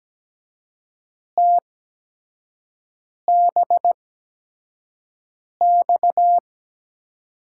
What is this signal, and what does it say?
Morse code, heard as a keyed tone, about 17 wpm, spaced as for 5 wpm, 700 Hz. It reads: TBX
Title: USA Call Signs - 8x Character Spacing - 17wpm